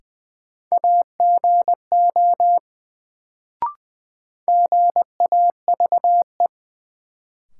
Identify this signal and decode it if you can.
Morse code, heard as a keyed tone, about 20 wpm, 700 Hz. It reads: AGO GAVE
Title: Sentences from Top 500 Words - New Word Drill - 20wpm